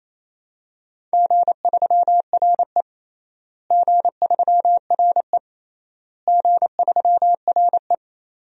Morse code, heard as a keyed tone, about 28 wpm, 700 Hz. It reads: G3RE G3RE G3RE
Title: DXCC-100 Call Signs - 4 Characters - Repeated 3x - 28wpm